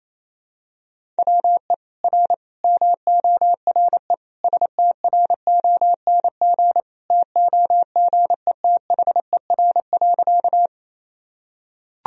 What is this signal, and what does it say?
Morse code, heard as a keyed tone, about 28 wpm, 700 Hz. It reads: WE R MORE STRONG TOGETHER.
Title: Sentences from Top 500 Words and Abbreviations 28wpm